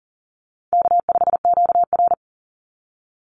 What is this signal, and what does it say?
Morse code, heard as a keyed tone, about 40 wpm, 700 Hz. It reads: K5QR
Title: NA SOTA Chaser Callsigns 40wpm